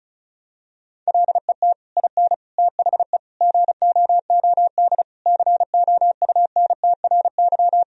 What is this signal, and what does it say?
Morse code, heard as a keyed tone, about 34 wpm, 700 Hz. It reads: LET IN THE GOOD COUNTRY
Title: Sentences from Top 1000 Words - Review Part 1 - 35wpm